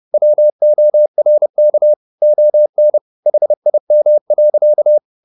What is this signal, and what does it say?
Morse code, heard as a keyed tone, about 30 wpm, 600 Hz. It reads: WORK ON HIM.